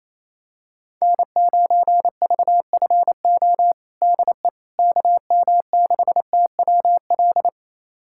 Morse code, heard as a keyed tone, about 28 wpm, 700 Hz. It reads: N9VFO DE KM6TWL
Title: USA Callsign DE Callsign 28wpm